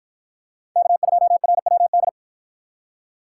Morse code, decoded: K2RWD

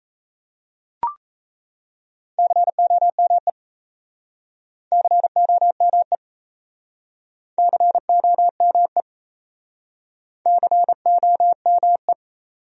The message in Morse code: COME COME COME COME

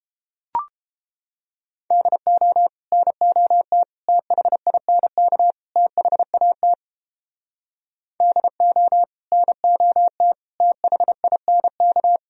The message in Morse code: DO NOT THINK THAT DO NOT THINK

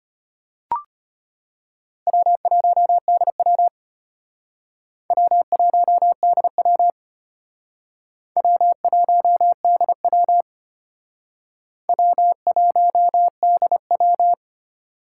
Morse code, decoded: W1DW W1DW W1DW W1DW

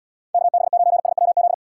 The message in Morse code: KB9SWZ

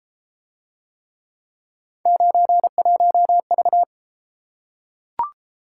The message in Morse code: 91V

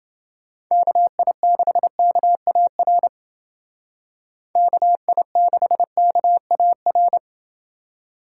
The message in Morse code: KI6KAR KI6KAR

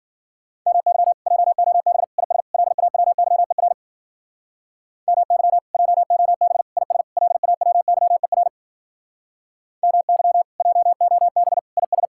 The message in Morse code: MY JOB IS LAWYER MY JOB IS LAWYER MY JOB IS

